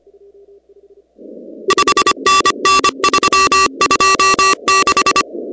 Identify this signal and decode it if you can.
Morse code, heard as a keyed tone, about 23 wpm, 1160 Hz, with noise. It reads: 5NN 326
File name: contests / 2019-cw-wpx-kt5j-k5pi- -2330 0-16